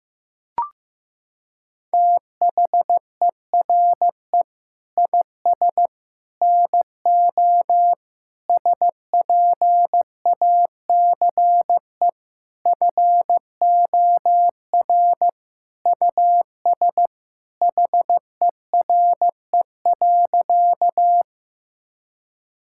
Morse code, heard as a keyed tone, about 15 wpm, 700 Hz. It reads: THERE IS NO SPACE FOR US HERE.
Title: Sentences from Top 1000 Words - Review Part 5 - 15wpm